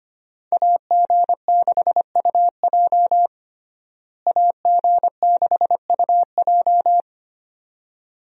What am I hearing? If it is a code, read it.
Morse code, heard as a keyed tone, about 25 wpm, 700 Hz. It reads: AG6UJ AG6UJ
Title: USA Call Signs - 5 Characters - Repeated 2x - 25wpm